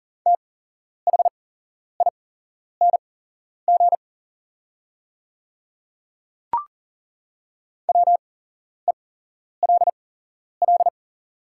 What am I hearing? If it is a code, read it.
Morse code, heard as a keyed tone, about 40 wpm, spaced as for 11 wpm, 700 Hz. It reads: THING WELL
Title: 8x Character Spacing - Top 500 Words - 40wpm